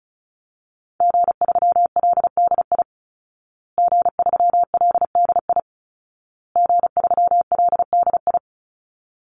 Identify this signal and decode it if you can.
Morse code, heard as a keyed tone, about 34 wpm, 700 Hz. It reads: G3LDI G3LDI G3LDI
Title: Call Signs Repeated 2x - 35wpm